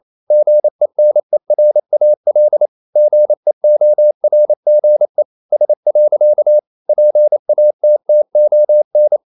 Morse code, heard as a keyed tone, about 28 wpm, 600 Hz. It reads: GENERAL GEORGE S. PATTON
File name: CW / Practice / EnglishQuotations0128